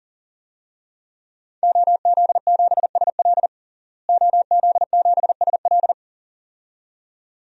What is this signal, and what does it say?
Morse code, heard as a keyed tone, about 40 wpm, 700 Hz. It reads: OZ7SL OZ7SL
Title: DXCC-100 Call Signs - 5 Characters - Repeated 2x - 40wpm